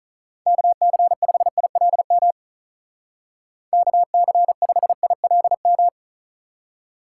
Morse code, heard as a keyed tone, about 45 wpm, 700 Hz. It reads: KC5ILM KC5ILM